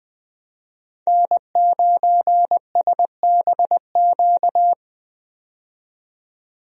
Morse code, heard as a keyed tone, about 20 wpm, 700 Hz. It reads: N9SBQ
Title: USA Call Signs - 5 Characters - 20wpm